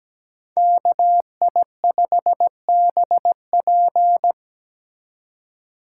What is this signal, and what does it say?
Morse code, heard as a keyed tone, about 20 wpm, 700 Hz. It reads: KI5BP
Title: Speed Racing - USA 2x2 Call Signs 17wpm